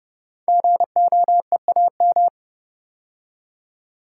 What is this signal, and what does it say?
Morse code, heard as a keyed tone, about 30 wpm, 700 Hz. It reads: GOEAM
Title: Code Groups - 5 Letters - 30wpm